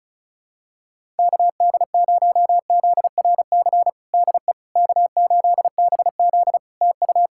Morse code, heard as a keyed tone, about 36 wpm, 700 Hz. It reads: KD0ZRC DE K8BZ TU
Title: Practice QSO - Exchange 4 - Ending - 35wpm